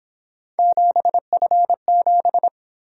Morse code, heard as a keyed tone, about 26 wpm, 700 Hz. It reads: 7F7
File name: Speed-Racing - Three Letter-Number Combinations - 22wpm